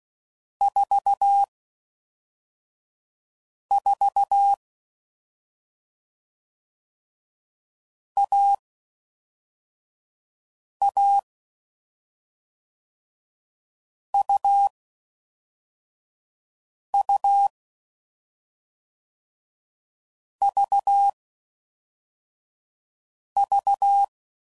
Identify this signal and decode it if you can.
Morse code, heard as a keyed tone, about 16 wpm, 800 Hz. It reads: 4 4 A A U U V V
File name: Mixed / Short Runs / AUV AUV4